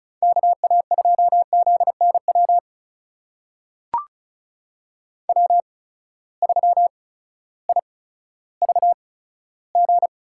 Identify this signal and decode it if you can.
Morse code, heard as a keyed tone, about 36 wpm, spaced as for 16 wpm, 700 Hz. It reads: KA2ZNW W3IVG
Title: USA Call Signs - 8x Character Spacing - 35wpm